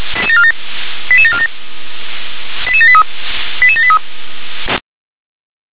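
Five-tone selective call (ZVEI), heard as quick runs of five tones, tones of about 70 ms. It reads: 88747 88783 88783
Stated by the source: Received via SDR as a ZVEI five-tone call, address 88641.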